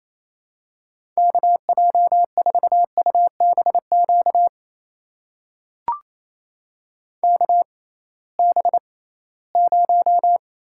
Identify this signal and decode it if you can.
Morse code, heard as a keyed tone, about 28 wpm, spaced as for 18 wpm, 700 Hz. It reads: KJ4UBQ KB0
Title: USA Call Signs - 6 Characters - 6x Character Spacing - 28wpm